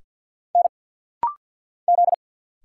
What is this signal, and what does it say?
Morse code, heard as a keyed tone, about 50 wpm, 700 Hz. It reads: N Z
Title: Single Letter-Number Alt - Rapid-Fire 50wpm